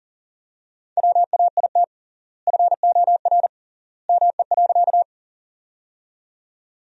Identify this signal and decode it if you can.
Morse code, heard as a keyed tone, about 40 wpm, 700 Hz. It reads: WAIT FOR ME.
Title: 3x Word-Spacing - Sentences from Top 500 Words - 40wpm